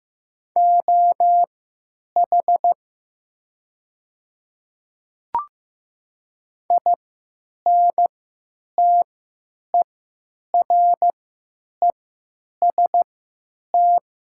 Morse code, heard as a keyed tone, about 15 wpm, spaced as for 9 wpm, 700 Hz. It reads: OH INTEREST
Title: 3x Character Spacing - Top 500 Words - 15wpm